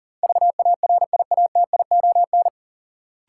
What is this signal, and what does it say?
Morse code, heard as a keyed tone, about 40 wpm, 700 Hz. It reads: VARIATION
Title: Sentences from Top 1600 Words - New Word Drill - 40wpm